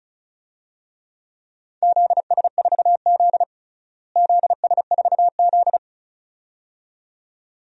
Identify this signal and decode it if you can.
Morse code, heard as a keyed tone, about 36 wpm, 700 Hz. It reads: ZS4Z ZS4Z
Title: DXCC-100 Call Signs - 4 Characters - Repeated 2x - 35wpm